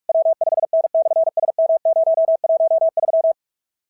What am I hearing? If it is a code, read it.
Morse code, heard as a keyed tone, about 46 wpm, 650 Hz. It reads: W5N/SM013
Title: SOTA QSO Practice 45wpm